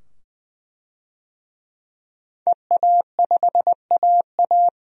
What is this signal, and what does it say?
Morse code, heard as a keyed tone, about 21 wpm, 700 Hz. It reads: EA5AA
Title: DXCC-100 Call Signs - 5 Characters - 20wpm